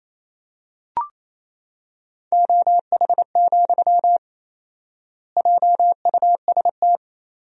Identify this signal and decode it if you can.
Morse code, heard as a keyed tone, about 28 wpm, 700 Hz. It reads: OH, JUST
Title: 4x Word-Spacing - Sentences from Top 500 Words - 28wpm